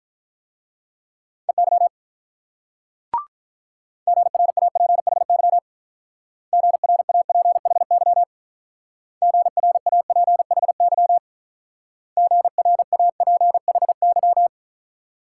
Morse code, heard as a keyed tone, about 46 wpm, 700 Hz. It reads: EY GRAPHY GRAPHY GRAPHY GRAPHY